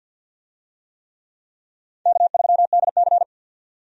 Morse code, heard as a keyed tone, about 52 wpm, 700 Hz. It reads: K3DC
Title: USA Call Signs - 4 Characters - 50wpm